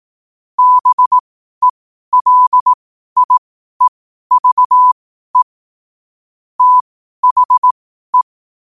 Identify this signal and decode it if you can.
Morse code, heard as a keyed tone, about 18 wpm, spaced as for 12 wpm, 1000 Hz. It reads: BELIEVE THE
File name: qotdfiles / 12wpm